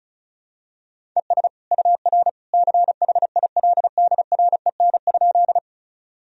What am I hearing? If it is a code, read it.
Morse code, heard as a keyed tone, about 36 wpm, 700 Hz. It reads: ES UR CHILDREN?
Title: Sentences from Top 300 Words and Abbreviations 35wpm